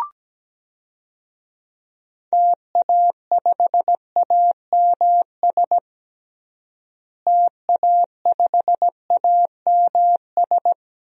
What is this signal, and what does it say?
Morse code, heard as a keyed tone, about 17 wpm, 700 Hz. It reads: TA5AMS TA5AMS
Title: DXCC-100 Call Signs - 6 Characters - Repeated 3x - 17wpm